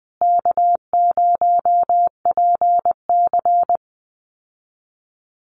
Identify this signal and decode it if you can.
Morse code, heard as a keyed tone, about 20 wpm, 700 Hz. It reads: K0PC
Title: Easy Call Signs Repeated 2x - 20wpm